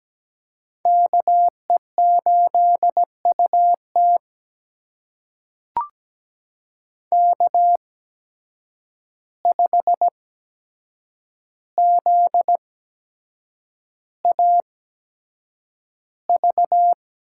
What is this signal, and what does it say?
Morse code, heard as a keyed tone, about 17 wpm, spaced as for 7 wpm, 700 Hz. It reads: KE8UT K5ZAV
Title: USA Call Signs - 5 Characters - 8x Character Spacing - 17wpm